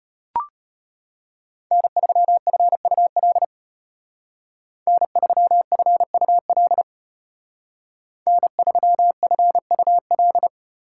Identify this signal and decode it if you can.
Morse code, heard as a keyed tone, about 30 wpm, 700 Hz. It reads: N3FUL N3FUL N3FUL
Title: Speed Racing - USA 1x3 Call Signs 25wpm